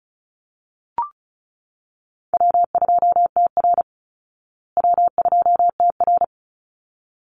Morse code, heard as a keyed tone, about 34 wpm, 700 Hz. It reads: W2TR W2TR
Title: Easy Call Signs Repeated 2x - 35wpm